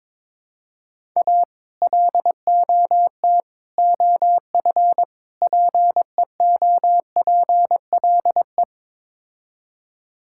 Morse code, heard as a keyed tone, about 22 wpm, 700 Hz. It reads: A LOT OF PEOPLE
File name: practice / 22wpm / Build A Phrase - 5-Word Phrases - 22wpm